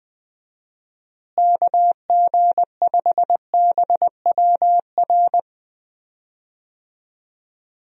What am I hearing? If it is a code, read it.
Morse code, heard as a keyed tone, about 21 wpm, 700 Hz. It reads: KG5BWR
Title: USA Call Signs - 6 Characters - Repeated 3x - 20wpm